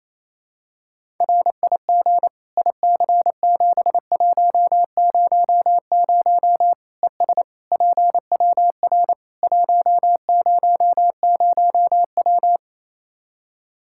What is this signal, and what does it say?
Morse code, heard as a keyed tone, about 28 wpm, 700 Hz. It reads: RIG IC7100 ES PWR 100W
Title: Practice QSO - Exchange 2 - Info - 28wpm